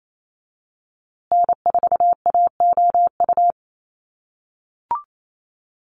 Morse code, heard as a keyed tone, about 28 wpm, 700 Hz. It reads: N4AOU
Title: USA Call Signs Repeated 2x 28wpm